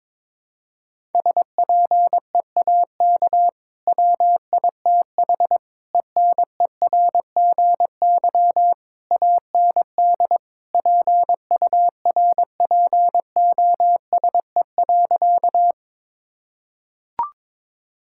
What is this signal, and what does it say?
Morse code, heard as a keyed tone, about 21 wpm, 700 Hz. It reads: SPEAK WITH ENERGY AND PURPOSE.
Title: Sentences from Top 1000 Words - Review Part 6 - 22wpm